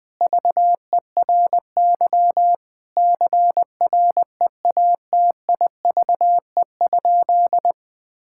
Morse code, heard as a keyed tone, about 20 wpm, 700 Hz. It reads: VERY CREATIVE?
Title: Sentences from Top 1000 Words - Review Part 5 - 20wpm